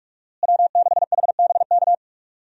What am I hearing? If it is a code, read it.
Morse code, heard as a keyed tone, about 43 wpm, 700 Hz. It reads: W6HBK
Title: USA Call Signs - 5 Characters - Repeated 3x - 45wpm